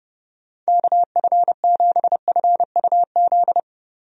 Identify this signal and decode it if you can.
Morse code, heard as a keyed tone, about 31 wpm, 700 Hz. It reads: KF7FUZ
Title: USA Call Signs - 6 Characters - 30wpm